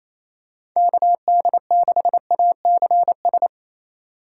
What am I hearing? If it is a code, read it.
Morse code, heard as a keyed tone, about 28 wpm, 700 Hz. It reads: KD6ACS
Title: USA Call Signs - 4x Character Spacing - 28wpm